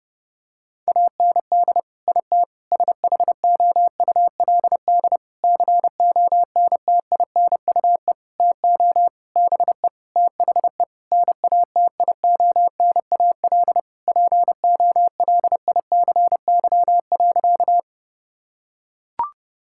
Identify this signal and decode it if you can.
Morse code, heard as a keyed tone, about 32 wpm, 700 Hz. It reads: AND IT SHOULD CONTINUE TO BE THE NATIONAL POLICY.
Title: Sentences from Top 1000 Words - Review Part 6 - 30wpm